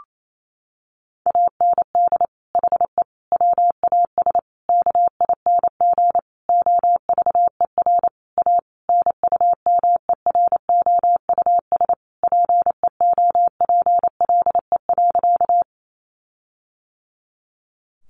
Morse code, heard as a keyed tone, about 28 wpm, 700 Hz. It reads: AND HE WAS KING OVER A NUMEROUS PEOPLE.
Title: Sentences from Top 1400 Words 28wpm